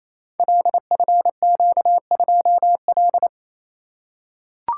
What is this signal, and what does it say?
Morse code, heard as a keyed tone, about 28 wpm, 700 Hz. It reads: LFQ2L
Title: Code Groups - 5 Letter:Numbers - 28wpm